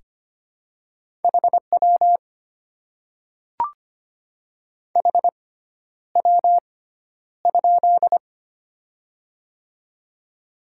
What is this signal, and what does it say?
Morse code, heard as a keyed tone, about 25 wpm, spaced as for 11 wpm, 700 Hz. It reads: HW HW?
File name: Top 90 QSO Elements - 6x Character Spacing - 25wpm